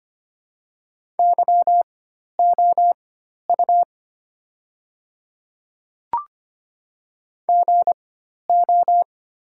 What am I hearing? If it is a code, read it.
Morse code, heard as a keyed tone, about 25 wpm, spaced as for 12 wpm, 700 Hz. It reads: YOU GO
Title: 4x Character Spacing - Top 100 Words - 25wpm